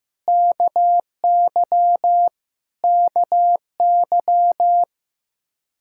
Morse code, heard as a keyed tone, about 15 wpm, 700 Hz. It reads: KY KY
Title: K1USN SST QSO Practice 15wpm